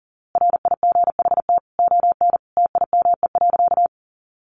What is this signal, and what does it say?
Morse code, heard as a keyed tone, about 40 wpm, 700 Hz. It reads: RIGHT ON TIME.
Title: A Sound Of Thunder 40wpm